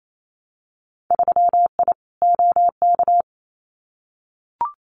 Morse code, heard as a keyed tone, about 28 wpm, 690 Hz. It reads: 3I OK
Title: Winter Field Day Exchanges Encore 28wpm